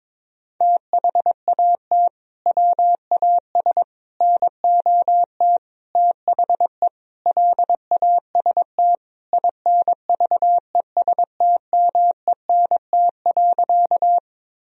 Morse code, heard as a keyed tone, about 22 wpm, 700 Hz. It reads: THAT WAS NOT THE LAST INVESTMENT.
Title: Sentences from Top 1000 Words - Review Part 4 - 22wpm